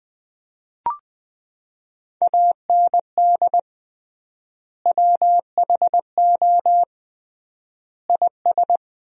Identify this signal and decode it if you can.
Morse code, heard as a keyed tone, about 20 wpm, 700 Hz. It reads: AND WHO IS
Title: 3x Word-Spacing - Sentences from Top 300 Words - 20wpm